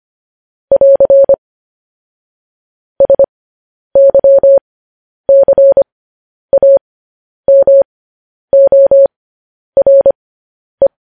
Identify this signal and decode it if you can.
Morse code, heard as a keyed tone, about 25 wpm, spaced as for 10 wpm, 550 Hz. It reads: <AR> SYCAMORE
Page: news-headlines-in-morse-code-at-10-wpm